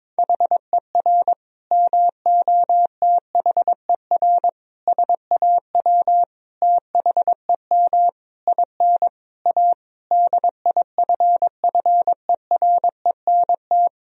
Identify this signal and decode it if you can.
Morse code, heard as a keyed tone, about 22 wpm, 700 Hz. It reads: HER MOTHER SAW THEM IN A DIFFERENT
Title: Sentences from Top 1000 Words - Review Part 3 - 22wpm